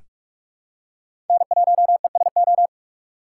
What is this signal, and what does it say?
Morse code, heard as a keyed tone, about 42 wpm, 700 Hz. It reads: N1ESO
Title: USA Call Signs - 5 Characters - 4x Character Spacing - 45wpm